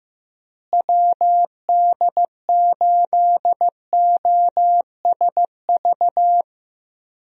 Morse code, heard as a keyed tone, about 15 wpm, 700 Hz. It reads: WD8OSV
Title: USA Call Signs - 6 Characters - Repeated 3x - 15wpm